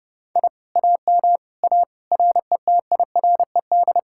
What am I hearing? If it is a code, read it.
Morse code, heard as a keyed tone, about 28 wpm, 700 Hz. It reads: I AM A RETIRED